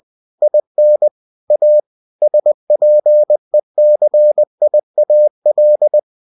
Morse code, heard as a keyed tone, about 20 wpm, 600 Hz. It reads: IN A SPECIAL
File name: cw-books / winnie-the-pooh / winnie-20wpm-page-0102